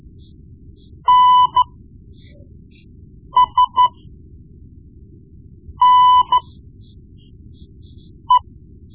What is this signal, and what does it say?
Morse code, heard as a keyed tone, about 13 wpm, spaced as for 4 wpm, 1000 Hz. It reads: NSNE
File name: Morse Lessons / Morse Lessons / Tape-1B